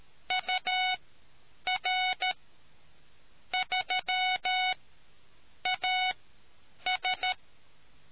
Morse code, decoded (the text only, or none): UR3AS